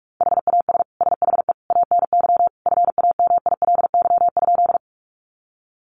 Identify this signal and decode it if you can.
Morse code, heard as a keyed tone, about 44 wpm, 700 Hz. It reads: HAS SHE ANY FAMILY?
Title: Sentences from Top 400 Words 45wpm